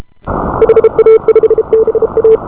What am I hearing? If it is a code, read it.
Morse code, heard as a keyed tone, about 34 wpm, 470 Hz. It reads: HA5BA